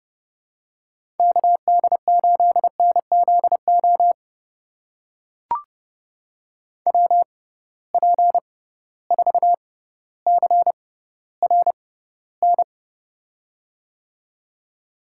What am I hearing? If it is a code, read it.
Morse code, heard as a keyed tone, about 32 wpm, spaced as for 16 wpm, 700 Hz. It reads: KD8NZO WP4CRN
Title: USA Call Signs - 6 Characters - 6x Character Spacing - 30wpm